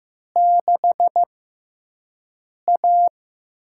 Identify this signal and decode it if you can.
Morse code, heard as a keyed tone, about 15 wpm, spaced as for 5 wpm, 700 Hz. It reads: 6A
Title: USA Call Signs - 4 Characters - 6x Character Spacing - 15wpm